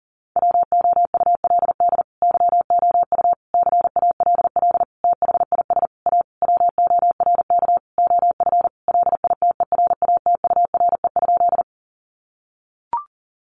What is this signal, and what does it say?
Morse code, heard as a keyed tone, about 40 wpm, 700 Hz. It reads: WOULD YOU CALL THIS A WORK OF LITERATURE?
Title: Sentences from Top 600 Words 40wpm